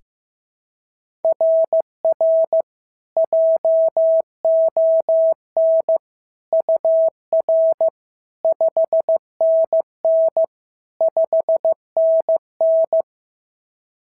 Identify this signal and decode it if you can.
Morse code, heard as a keyed tone, about 15 wpm, 690 Hz. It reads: RR JON UR 5NN 5NN
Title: SOTA QSO Practice 15wpm